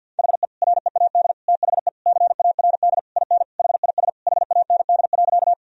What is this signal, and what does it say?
Morse code, heard as a keyed tone, about 50 wpm, 700 Hz. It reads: HE READ THE CARD IN HIS HAND.
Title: Sentences from Top 1000 Words - Review Part 4 - 50wpm